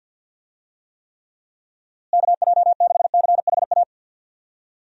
Morse code, heard as a keyed tone, about 46 wpm, 700 Hz. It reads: KJ6CHA
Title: USA Call Signs - 6 Characters - Repeated 2x - 50wpm